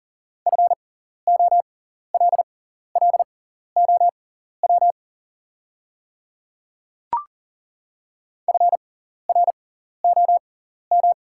Morse code, heard as a keyed tone, about 38 wpm, spaced as for 14 wpm, 690 Hz. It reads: FOLLOW FROM
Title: 6x Character Spacing - Top 200 Words - 40wpm